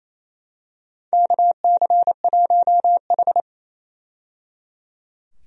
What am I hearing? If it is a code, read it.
Morse code, heard as a keyed tone, about 28 wpm, 700 Hz. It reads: KC1H
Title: USA Call Signs - 4 Characters - 28wpm